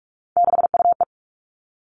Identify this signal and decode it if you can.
Morse code, heard as a keyed tone, about 46 wpm, 700 Hz. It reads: 6UE
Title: Three Challenging Letter-Number Combinations 45wpm